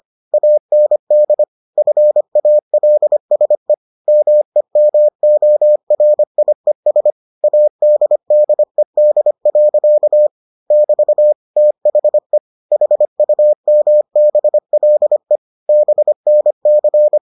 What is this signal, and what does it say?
Morse code, heard as a keyed tone, about 25 wpm, 600 Hz. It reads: AND FALSE MEMORIES ADDED. <BT> THE HUMBLE BNC